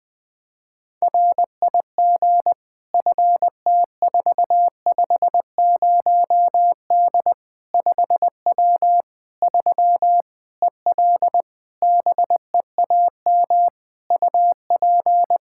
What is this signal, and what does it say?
Morse code, heard as a keyed tone, about 20 wpm, 700 Hz. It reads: RIG FT450D 5W 3 EL BEAM UP
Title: Practice QSO - Exchange 2 - Info - 20wpm